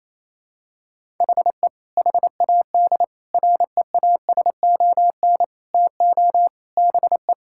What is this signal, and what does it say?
Morse code, heard as a keyed tone, about 27 wpm, 700 Hz. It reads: HE HAD REASON TO BE
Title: Sentences from Top 1000 Words - Review Part 6 - 28wpm